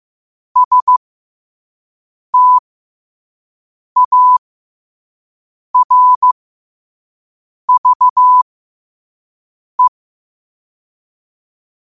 Morse code, heard as a keyed tone, about 15 wpm, spaced as for 5 wpm, 1000 Hz. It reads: STARVE
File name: qotdfiles / 5wpm